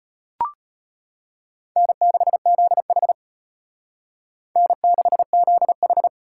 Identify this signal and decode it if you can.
Morse code, heard as a keyed tone, about 33 wpm, 700 Hz. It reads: N6ZH N6ZH